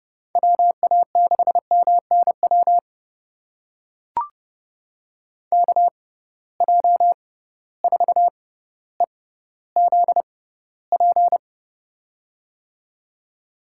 Morse code, heard as a keyed tone, about 30 wpm, spaced as for 15 wpm, 700 Hz. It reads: WA6MNW KJ4EZP
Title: USA Call Signs - 6 Characters - 6x Character Spacing - 30wpm